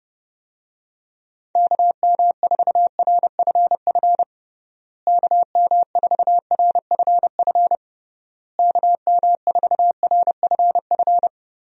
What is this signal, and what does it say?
Morse code, heard as a keyed tone, about 30 wpm, 700 Hz. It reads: KM4RFF KM4RFF KM4RFF
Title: USA Call Signs - 6 Characters - Repeated 3x - 30wpm